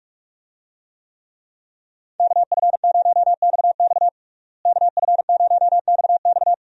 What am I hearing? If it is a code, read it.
Morse code, heard as a keyed tone, about 46 wpm, 700 Hz. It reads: KF0XX KF0XX
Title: USA Call Signs - 5 Characters - Repeated 2x - 45wpm